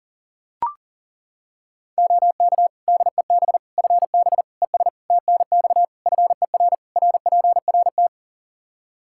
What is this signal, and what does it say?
Morse code, heard as a keyed tone, about 37 wpm, 700 Hz. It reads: OK DEB FB ES TNX FER RPRT
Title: Practice QSO - Exchange 2 - Info - 40wpm